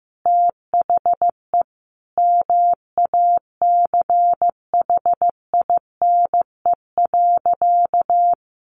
Morse code, heard as a keyed tone, about 15 wpm, 700 Hz. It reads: THE MACHINE.
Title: A Sound Of Thunder 15wpm